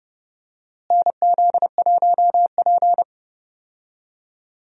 Morse code, heard as a keyed tone, about 30 wpm, 700 Hz. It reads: NZ1P
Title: USA Call Signs - 4 Characters - Repeated 3x - 30wpm